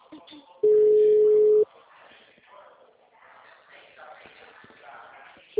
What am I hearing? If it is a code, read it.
Ringback tone (Europe, 425 Hz), with speech.